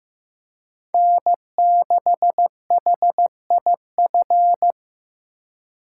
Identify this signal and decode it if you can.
Morse code, heard as a keyed tone, about 15 wpm, 700 Hz. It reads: N6HIF
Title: USA Call Signs - 5 Characters - 4x Character Spacing - 15wpm